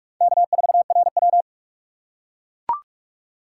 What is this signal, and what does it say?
Morse code, heard as a keyed tone, about 44 wpm, 700 Hz. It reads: K4RW
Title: USA Call Signs - 4 Characters - 6x Character Spacing - 45wpm